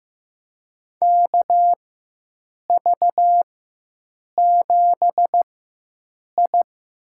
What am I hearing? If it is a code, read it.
Morse code, heard as a keyed tone, about 15 wpm, spaced as for 7 wpm, 700 Hz. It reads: KV7I
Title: USA Call Signs - 4 Characters - 4x Character Spacing - 15wpm